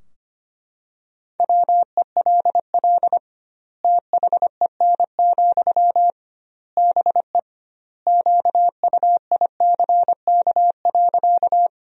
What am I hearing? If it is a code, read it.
Morse code, heard as a keyed tone, about 25 wpm, 700 Hz. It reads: WELL THEN, BE QUICK.
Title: 2x Word-Spacing - Sentences from Top 500 Words - 25wpm